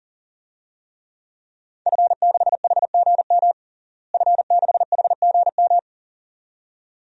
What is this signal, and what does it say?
Morse code, heard as a keyed tone, about 40 wpm, 700 Hz. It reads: F6HGM F6HGM
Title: DXCC-100 Call Signs - 5 Characters - Repeated 2x - 40wpm